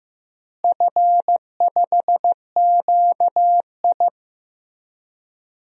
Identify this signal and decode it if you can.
Morse code, heard as a keyed tone, about 15 wpm, 700 Hz. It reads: F5QI
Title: DXCC-100 Call Signs - 4 Characters - Repeated 3x - 15wpm